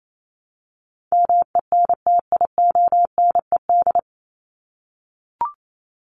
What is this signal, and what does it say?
Morse code, heard as a keyed tone, about 28 wpm, 700 Hz. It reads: MENTIONED
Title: ED Suffix - Top 100 Words with Base Words 28wpm